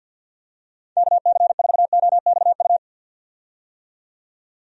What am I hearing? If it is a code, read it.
Morse code, heard as a keyed tone, about 52 wpm, 700 Hz. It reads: KC4OXU
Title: USA Call Signs - 6 Characters - 50wpm